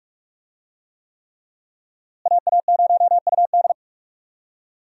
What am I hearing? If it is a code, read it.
Morse code, heard as a keyed tone, about 46 wpm, 700 Hz. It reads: AA0UD
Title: USA Call Signs - 5 Characters - Repeated 3x - 45wpm